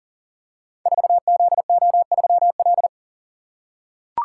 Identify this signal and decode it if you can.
Morse code, heard as a keyed tone, about 43 wpm, 690 Hz. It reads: 4ZO3L
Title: Code Groups - 5 Letter:Numbers - 40wpm